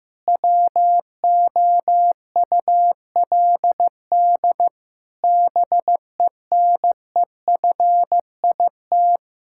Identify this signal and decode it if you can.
Morse code, heard as a keyed tone, about 15 wpm, 700 Hz. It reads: WOULD BENEFIT
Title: Sentences from Top 1000 Words - Review Part 6 - 15wpm